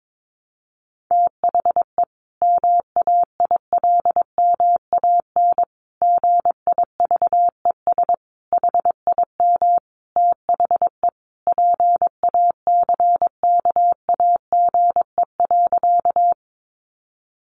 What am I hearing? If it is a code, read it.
Morse code, heard as a keyed tone, about 22 wpm, 700 Hz. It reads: THE MAILMAN GIVES HIM THE PACKAGE.
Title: Easy Phrases 22wpm